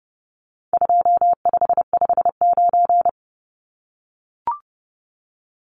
Morse code, heard as a keyed tone, about 30 wpm, 700 Hz. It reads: 2559
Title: Numbers - 4-Digit 30wpm